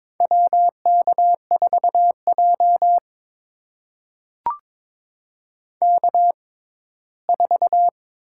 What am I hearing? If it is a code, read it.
Morse code, heard as a keyed tone, about 22 wpm, spaced as for 15 wpm, 700 Hz. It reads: WK4J K4
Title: USA Call Signs - 4 Characters - 6x Character Spacing - 22wpm